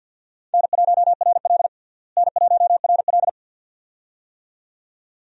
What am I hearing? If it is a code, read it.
Morse code, heard as a keyed tone, about 48 wpm, 700 Hz. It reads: N1RL N1RL
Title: USA Call Signs - 4 Characters - Repeated 2x - 50wpm